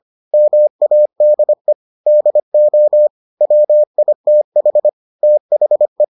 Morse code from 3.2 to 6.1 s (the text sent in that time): WITH THE